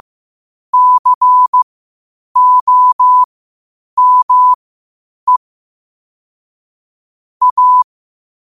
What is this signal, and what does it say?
Morse code, heard as a keyed tone, about 15 wpm, spaced as for 8 wpm, 1000 Hz. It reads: COME A